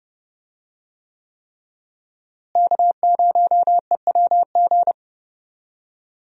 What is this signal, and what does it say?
Morse code, heard as a keyed tone, about 30 wpm, 700 Hz. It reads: K0EWG
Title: USA Call Signs - 5 Characters - Repeated 3x - 30wpm